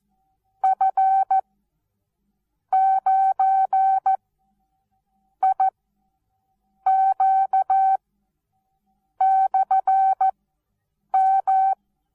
Morse code, heard as a keyed tone, about 14 wpm, spaced as for 6 wpm, 770 Hz. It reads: F9IQ/M
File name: Onlinelehrgang / morsen / DJ4UF Morsen19